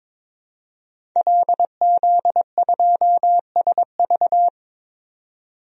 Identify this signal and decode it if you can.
Morse code, heard as a keyed tone, about 22 wpm, 700 Hz. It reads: LZ2SV
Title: DXCC-100 Call Signs - 5 Characters - 22wpm